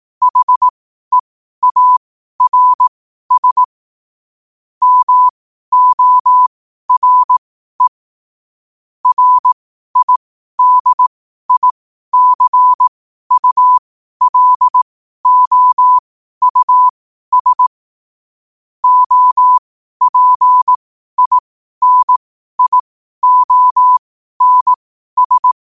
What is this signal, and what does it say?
Morse code, heard as a keyed tone, about 18 wpm, spaced as for 12 wpm, 1000 Hz. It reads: HEARS MORE RIDICULOUS OPINIONS
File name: qotdfiles / 12wpm